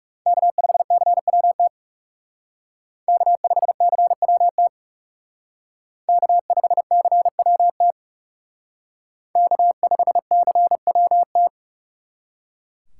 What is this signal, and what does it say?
Morse code, heard as a keyed tone, about 36 wpm, 700 Hz. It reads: K5CWT K5CWT K5CWT K5CWT
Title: Speed Racing - USA Call Signs 30wpm